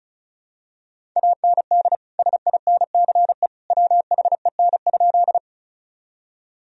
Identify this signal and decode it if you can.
Morse code, heard as a keyed tone, about 34 wpm, 700 Hz. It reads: AND SINCE WHEN?
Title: Sentences from Top 1000 Words - Review Part 1 - 35wpm